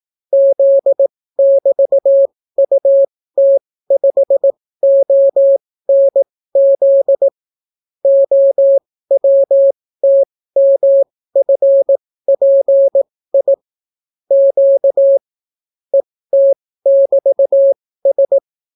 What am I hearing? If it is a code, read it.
Morse code, heard as a keyed tone, about 18 wpm, spaced as for 15 wpm, 550 Hz. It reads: Z<BT>UT5ONZ OWTMFPI Q ET<BT>S